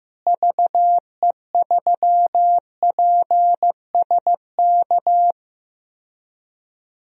Morse code, heard as a keyed tone, about 15 wpm, 700 Hz. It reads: VE3PSK
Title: DXCC-100 Call Signs - 6 Characters - Repeated 3x - 15wpm